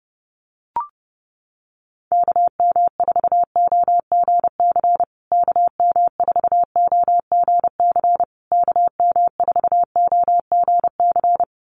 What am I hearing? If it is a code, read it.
Morse code, heard as a keyed tone, about 30 wpm, 700 Hz. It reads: KM4OGC KM4OGC KM4OGC
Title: USA Call Signs Repeated 3x 30wpm